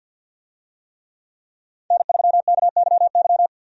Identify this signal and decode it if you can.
Morse code, heard as a keyed tone, about 53 wpm, 700 Hz. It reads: N3KYY
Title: USA Call Signs - 5 Characters - 50wpm